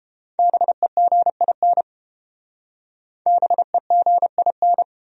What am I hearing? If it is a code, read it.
Morse code, heard as a keyed tone, about 29 wpm, 700 Hz. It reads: BEGIN BEGIN